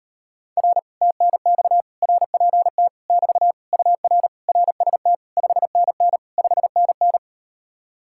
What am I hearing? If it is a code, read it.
Morse code, heard as a keyed tone, about 38 wpm, 700 Hz. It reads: R TNX RPT <BT> UR RST 5NN 5NN